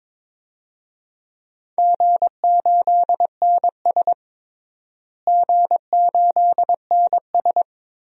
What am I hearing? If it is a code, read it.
Morse code, heard as a keyed tone, about 22 wpm, 700 Hz. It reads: G8NS G8NS
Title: DXCC-100 Call Signs - 4 Characters - Repeated 2x - 22wpm